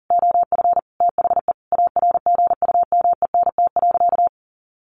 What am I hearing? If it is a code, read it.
Morse code, heard as a keyed tone, about 40 wpm, 700 Hz. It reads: OF THE ARGUMENT.
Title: Sentences from Top 1400 Words 40wpm